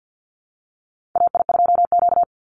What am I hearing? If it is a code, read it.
Morse code, heard as a keyed tone, about 50 wpm, 700 Hz. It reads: AI2Q
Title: Call Signs Repeated 3x - 50wpm